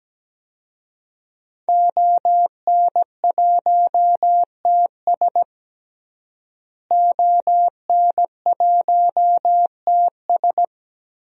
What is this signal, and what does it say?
Morse code, heard as a keyed tone, about 17 wpm, 700 Hz. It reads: ON1TS ON1TS
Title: DXCC-100 Call Signs - 5 Characters - Repeated 2x - 17wpm